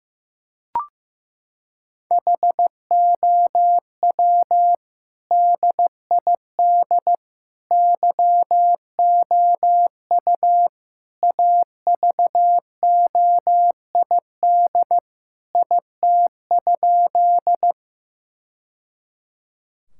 Morse code, heard as a keyed tone, about 15 wpm, 700 Hz. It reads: HOW DID YOU AVOID IT?
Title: Sentences from Top 1000 Words - Review Part 6 - 15wpm